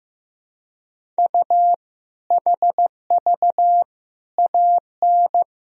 Morse code, heard as a keyed tone, about 15 wpm, 700 Hz. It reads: U HV AN
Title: Sentences from Top 400 Words and Abbreviations 15wpm